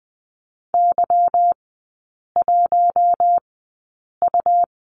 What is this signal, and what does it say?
Morse code, heard as a keyed tone, about 20 wpm, spaced as for 8 wpm, 700 Hz. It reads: Y1U
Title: 191 Lesson 31e - Letter v - Three Characters 20wpm